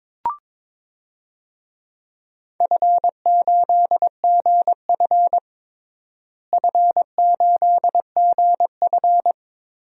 Morse code, heard as a keyed tone, about 22 wpm, 700 Hz. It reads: F8GF F8GF
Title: DXCC-100 Call Signs - 4 Characters - Repeated 3x - 22wpm